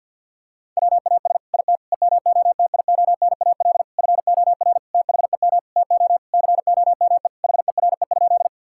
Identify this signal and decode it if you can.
Morse code, heard as a keyed tone, about 49 wpm, 700 Hz. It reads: WAS IT EMOTIONAL FOR THEM TO COME HERE?
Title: Sentences from Top 1000 Words - Review Part 5 - 50wpm